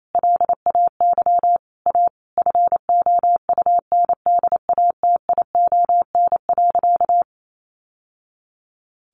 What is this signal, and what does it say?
Morse code, heard as a keyed tone, about 28 wpm, 700 Hz. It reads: LAY A FOUNDATION.
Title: Sentences from Top 1200 Words 28wpm